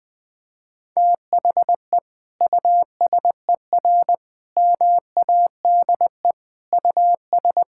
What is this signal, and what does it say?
Morse code, heard as a keyed tone, about 20 wpm, 700 Hz. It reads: THE USER MADE US
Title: Sentences from Top 1000 Words - Review Part 5 - 20wpm